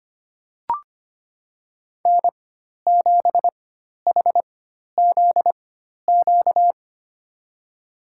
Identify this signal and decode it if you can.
Morse code, heard as a keyed tone, about 25 wpm, spaced as for 12 wpm, 700 Hz. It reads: N7HZQ
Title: USA Call Signs - 4x Character Spacing - 25wpm